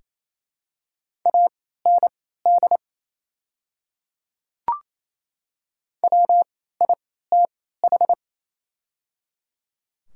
Morse code, heard as a keyed tone, about 27 wpm, spaced as for 16 wpm, 700 Hz. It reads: AND WITH
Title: 3x Character Spacing - Top 200 Words - 28wpm